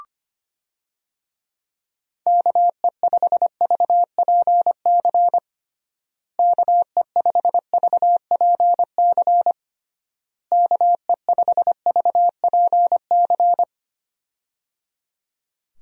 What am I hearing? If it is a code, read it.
Morse code, heard as a keyed tone, about 25 wpm, 700 Hz. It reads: KE5VPC KE5VPC KE5VPC
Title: USA Call Signs - 6 Characters - Repeated 3x - 25wpm